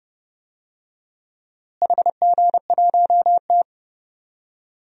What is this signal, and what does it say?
Morse code, heard as a keyed tone, about 30 wpm, 700 Hz. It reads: HG1T
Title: DXCC-100 Call Signs - 4 Characters - Repeated 3x - 30wpm